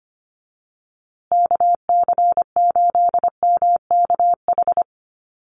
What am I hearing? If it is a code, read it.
Morse code, heard as a keyed tone, about 25 wpm, 700 Hz. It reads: KC8MKH
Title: USA Call Signs Repeated 2x 25wpm